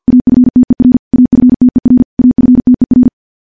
Morse code, heard as a keyed tone, about 50 wpm, 250 Hz. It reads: AFTER AFTER AFTER
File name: morse / morse-training-50WPM